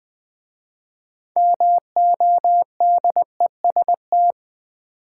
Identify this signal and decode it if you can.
Morse code, heard as a keyed tone, about 20 wpm, 700 Hz. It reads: MODEST
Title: 6 Letter Words - First 3 Letters Given - 20wpm